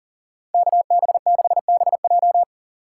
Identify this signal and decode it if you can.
Morse code, heard as a keyed tone, about 38 wpm, 700 Hz. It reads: KB6BJ
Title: USA Call Signs - 6x Character Spacing - 40wpm